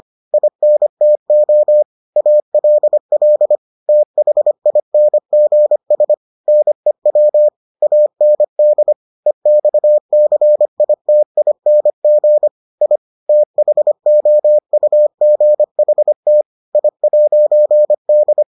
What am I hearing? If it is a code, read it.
Morse code, heard as a keyed tone, about 25 wpm, 600 Hz. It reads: INTO ALL THINGS NEW AND EXCITING I THOUGHT I'D